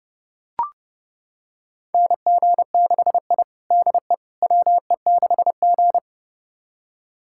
Morse code, heard as a keyed tone, about 30 wpm, 700 Hz. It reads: NG6I DE WE6G
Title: USA Callsign DE Callsign - 4 Characters - 30wpm